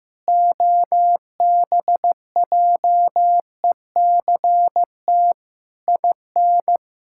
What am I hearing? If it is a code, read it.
Morse code, heard as a keyed tone, about 14 wpm, 700 Hz. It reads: OBJECT IN
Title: Sentences from Top 1000 Words - Review Part 6 - 15wpm